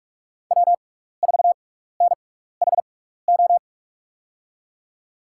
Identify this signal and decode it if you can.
Morse code, heard as a keyed tone, about 46 wpm, spaced as for 16 wpm, 700 Hz. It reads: W4NHO
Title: USA Call Signs - 5 Characters - 6x Character Spacing - 45wpm